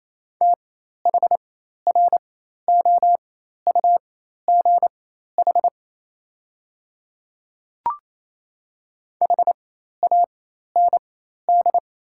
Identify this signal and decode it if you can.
Morse code, heard as a keyed tone, about 28 wpm, spaced as for 13 wpm, 700 Hz. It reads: THROUGH HAND
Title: 4x Character Spacing - Top 200 Words - 28wpm